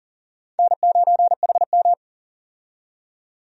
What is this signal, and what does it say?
Morse code, heard as a keyed tone, about 40 wpm, 700 Hz. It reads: N9HM
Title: USA Call Signs - 4 Characters - 40wpm